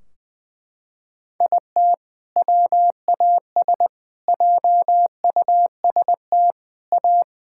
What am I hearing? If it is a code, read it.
Morse code, heard as a keyed tone, about 21 wpm, 700 Hz. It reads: IT WAS JUST A
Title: Sentences from Top 1600 Words 20wpm